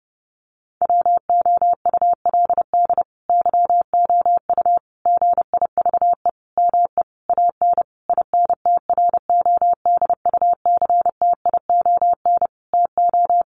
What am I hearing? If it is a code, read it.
Morse code, heard as a keyed tone, about 30 wpm, 700 Hz. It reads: WOULD YOU GIVE ME AN INTRODUCTION TO
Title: Sentences from Top 1000 Words 30wpm